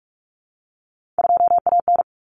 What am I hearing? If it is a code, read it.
Morse code, heard as a keyed tone, about 44 wpm, 700 Hz. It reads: 2AN
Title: Common Cut Numbers - 3-Digit 45wpm